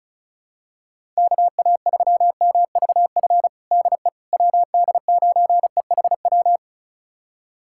Morse code, heard as a keyed tone, about 34 wpm, 700 Hz. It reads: KA3MVF DE WD9EHW
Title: USA Callsign DE Callsign 35wpm